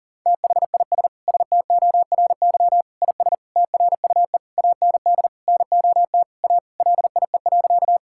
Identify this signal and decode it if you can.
Morse code, heard as a keyed tone, about 40 wpm, 700 Hz. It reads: THIS STORY IS TRUE AND NOT A LIE.
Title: Sentences from Top 1600 Words 40wpm